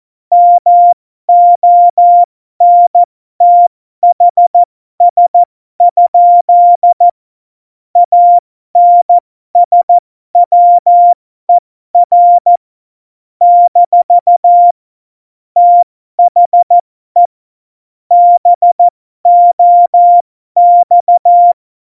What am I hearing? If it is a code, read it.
Morse code, heard as a keyed tone, about 14 wpm, spaced as for 12 wpm, 700 Hz. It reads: MONTHS? ANSWER - THE BOX
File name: Morsekurs / Witze